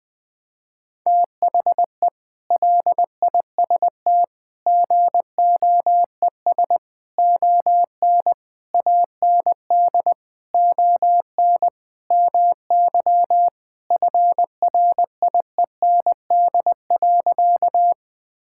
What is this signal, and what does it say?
Morse code, heard as a keyed tone, about 21 wpm, 700 Hz. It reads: THE LIST GOES ON AND ON MY FRIEND.
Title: Sentences with 6-Word Phrases - 20wpm